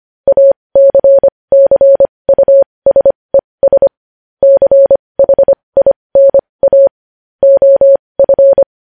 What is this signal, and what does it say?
Morse code, heard as a keyed tone, about 25 wpm, spaced as for 20 wpm, 550 Hz. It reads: ACCUSES CHINA OF
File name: podcast / News headlines-1x-25x20